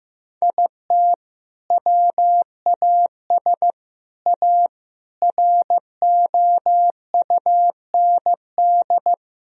Morse code, heard as a keyed tone, about 15 wpm, 700 Hz. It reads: IT WAS A ROUND
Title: Sentences from Top 1000 Words - Review Part 2 - 15wpm